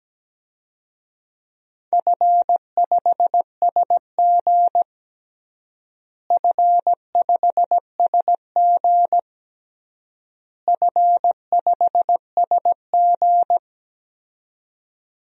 Morse code, heard as a keyed tone, about 17 wpm, 700 Hz. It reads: F5SG F5SG F5SG
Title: DXCC-100 Call Signs - 4 Characters - Repeated 3x - 17wpm